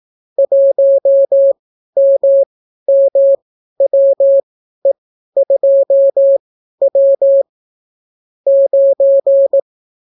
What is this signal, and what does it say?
Morse code, heard as a keyed tone, about 18 wpm, spaced as for 12 wpm, 550 Hz. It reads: 1MMWE2W 9